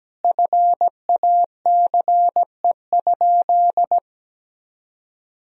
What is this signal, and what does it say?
Morse code, heard as a keyed tone, about 17 wpm, 700 Hz. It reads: FACE?
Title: Sentences from Top 1000 Words - Review Part 2 - 17wpm